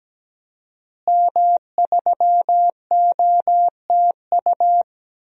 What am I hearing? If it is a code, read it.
Morse code, heard as a keyed tone, about 17 wpm, 700 Hz. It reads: M3OTU
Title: DXCC-100 Call Signs - 5 Characters - Repeated 3x - 17wpm